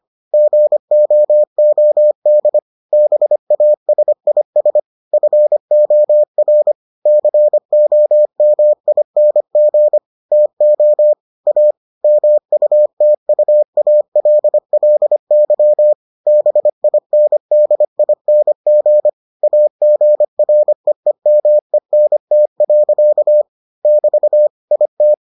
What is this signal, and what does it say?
Morse code, heard as a keyed tone, about 25 wpm, 600 Hz. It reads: GOOD BASIS FOR COMING TO A MUTUALLY BINDING AGREEMENT. <BT> IT